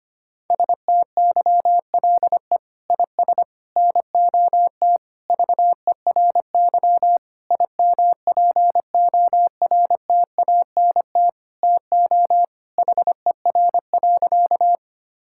Morse code, heard as a keyed tone, about 25 wpm, 700 Hz. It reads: STYLE IS NOT VERY IMPORTANT TO HER.
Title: Sentences from Top 1000 Words - Review Part 5 - 25wpm